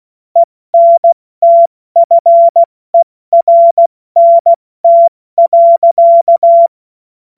Morse code, heard as a keyed tone, about 16 wpm, 680 Hz. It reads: ENTFERNT.
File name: Morsekurs / Witze / Witz 13. Juli 2021, 70 F80